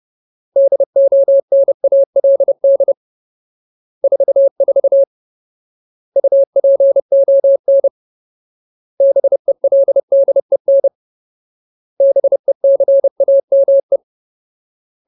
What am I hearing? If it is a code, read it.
Morse code, heard as a keyed tone, about 30 wpm, 550 Hz. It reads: DONALD 44 UPON BELDEN BECAME
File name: Ultimate Word List / 30WPM Ultimate